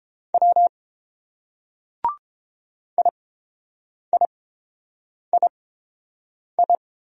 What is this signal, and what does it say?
Morse code, heard as a keyed tone, about 30 wpm, spaced as for 7 wpm, 700 Hz. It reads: W IIII